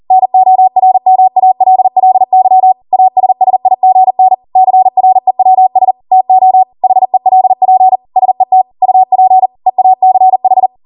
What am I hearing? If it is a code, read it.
Morse code, heard as a keyed tone, about 40 wpm, 750 Hz. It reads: NORMALLY ASSIGN CREWS TO HELP SET UP EACH